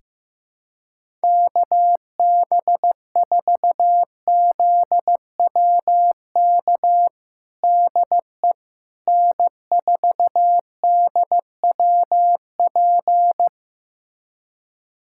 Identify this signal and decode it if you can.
Morse code, heard as a keyed tone, about 15 wpm, 700 Hz. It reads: KB4ZWK DE N4DWP
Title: USA Callsign DE Callsign 15wpm